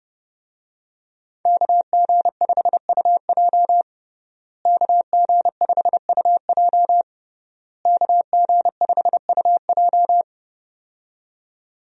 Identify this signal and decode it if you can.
Morse code, heard as a keyed tone, about 32 wpm, 700 Hz. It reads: KG5UJ KG5UJ KG5UJ
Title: USA Call Signs - 5 Characters - Repeated 3x - 30wpm